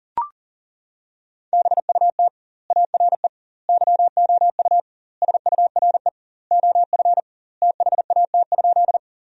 Morse code, heard as a keyed tone, about 40 wpm, 700 Hz. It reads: BUT ARE YOU SURE OF THAT?
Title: 2x Word-Spacing - Sentences from Top 300 Words - 40wpm